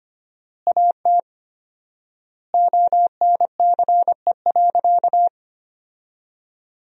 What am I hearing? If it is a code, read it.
Morse code, heard as a keyed tone, about 25 wpm, 700 Hz. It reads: AT ONCE.
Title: 4x Word-Spacing - Sentences from Top 500 Words - 25wpm